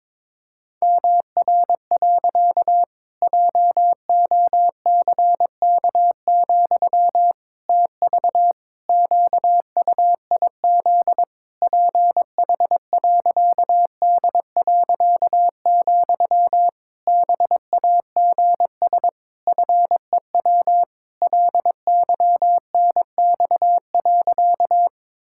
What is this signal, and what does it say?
Morse code, heard as a keyed tone, about 22 wpm, 700 Hz. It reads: MR. JOCK, TV QUIZ PH.D., BAGS FEW LYNX.
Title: Pangram Challenge v2 - 22wpm